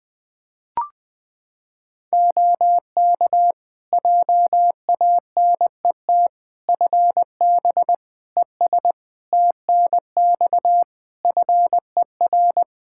Morse code, heard as a keyed tone, about 20 wpm, 700 Hz. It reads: OK JANET FB ES TNX FER
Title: Practice QSO - Exchange 2 - Info - 20wpm